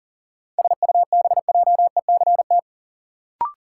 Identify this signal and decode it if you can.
Morse code, heard as a keyed tone, about 40 wpm, 700 Hz. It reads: SUBJECT
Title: Sentences from Top 800 Words - New Word Drill - 40wpm